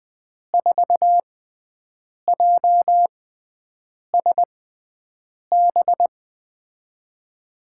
Morse code, heard as a keyed tone, about 20 wpm, spaced as for 7 wpm, 700 Hz. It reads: 4JSB
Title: USA Call Signs - 6 Characters - 6x Character Spacing - 20wpm